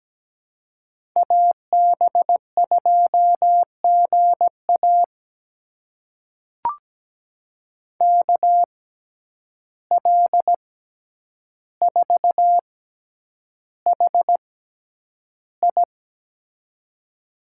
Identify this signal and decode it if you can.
Morse code, heard as a keyed tone, about 17 wpm, spaced as for 9 wpm, 700 Hz. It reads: AB2GA KL4HI
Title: USA Call Signs - 5 Characters - 6x Character Spacing - 17wpm